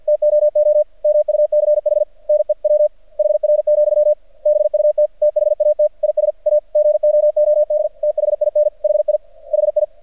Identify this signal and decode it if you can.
Morse code, heard as a keyed tone, about 49 wpm, 600 Hz. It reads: TOO MUCH NEWHR, BUT THAT IS A GOOD THIN HI HI
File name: archived / CRSnet